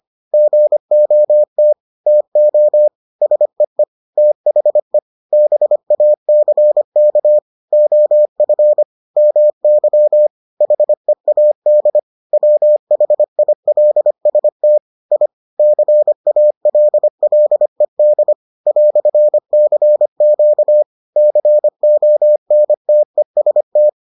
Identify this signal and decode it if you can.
Morse code, heard as a keyed tone, about 25 wpm, 600 Hz. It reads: GOT TO SEE THE BACK OF MY HEAD WHILST I CALLED "CQ CONTEST